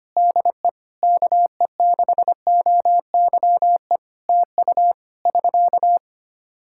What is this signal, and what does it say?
Morse code, heard as a keyed tone, about 25 wpm, 700 Hz. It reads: DE KE6OYE TU <SK>
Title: Practice QSO - Exchange 4 - Ending - 25wpm